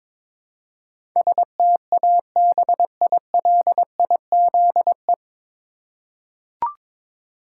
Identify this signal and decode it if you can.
Morse code, heard as a keyed tone, about 22 wpm, 700 Hz. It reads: STABILIZE
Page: Morse Code Ninja practice 22wpm